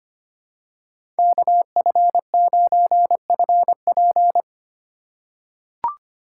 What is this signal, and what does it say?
Morse code, heard as a keyed tone, about 25 wpm, 700 Hz. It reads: KF9FP
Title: USA Call Signs - 4x Character Spacing - 25wpm